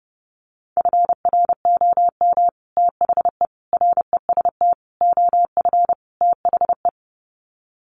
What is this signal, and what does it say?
Morse code, heard as a keyed tone, about 30 wpm, 700 Hz. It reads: FROM THE REST OF THE
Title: Sets of 5 Words 30wpm